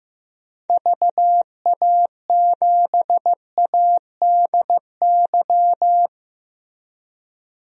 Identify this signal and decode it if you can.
Morse code, heard as a keyed tone, about 15 wpm, 700 Hz. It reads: VA7ADY
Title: DXCC-100 Call Signs - 6 Characters - 15wpm